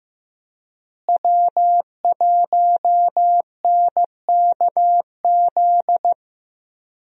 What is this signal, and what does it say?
Morse code, heard as a keyed tone, about 15 wpm, 700 Hz. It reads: W1NKZ
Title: USA Call Signs - 5 Characters - Repeated 3x - 15wpm